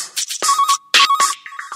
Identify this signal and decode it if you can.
There is no code.